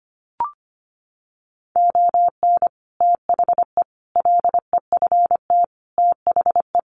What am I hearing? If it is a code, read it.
Morse code, heard as a keyed tone, about 25 wpm, 700 Hz. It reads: ON THE LEFT THE
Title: Sentences from Top 700 Words 25wpm